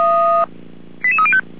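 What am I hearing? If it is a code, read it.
Five-tone selective call (ZVEI), heard as quick runs of five tones, tones of about 70 ms. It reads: 88305
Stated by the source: ZVEI five-tone call 88124 received by SDR, logged by feuerwehr-orlamuende.org fire brigade server.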